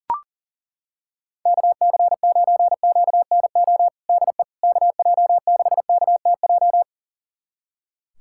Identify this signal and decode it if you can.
Morse code, heard as a keyed tone, about 40 wpm, 700 Hz. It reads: KC9QNO DE KJ6KTJ
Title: USA Callsign DE Callsign 40wpm